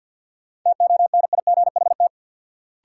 Morse code, heard as a keyed tone, about 53 wpm, 700 Hz. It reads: TONIGHT
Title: Sentences from Top 600 Words - New Word Drill - 50wpm